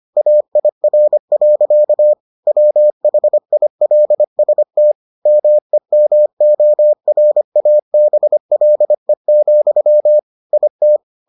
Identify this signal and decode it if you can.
Morse code, heard as a keyed tone, about 25 wpm, 600 Hz. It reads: AIR. WHILST MEMORABLE, IT